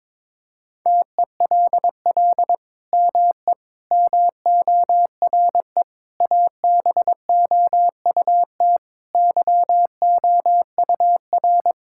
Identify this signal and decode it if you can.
Morse code, heard as a keyed tone, about 22 wpm, 700 Hz. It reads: TELL ME MORE ABOUT YOUR
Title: Sentences from Top 1000 Words - Review Part 5 - 22wpm